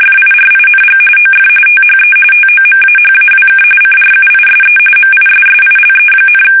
RTTY, about 45.45 baud. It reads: FIRING OF THE TOMAHAWK CRUISE MISSILE,
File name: fsk850 / battleshipnj